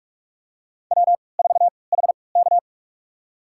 Morse code, heard as a keyed tone, about 46 wpm, spaced as for 25 wpm, 700 Hz. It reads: W4HK
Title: USA Call Signs - 3x Character Spacing - 45wpm